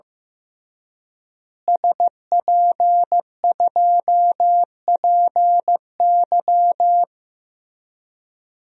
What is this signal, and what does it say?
Morse code, heard as a keyed tone, about 15 wpm, 700 Hz. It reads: SP2PY
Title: DXCC-100 Call Signs - 5 Characters - Repeated 2x - 15wpm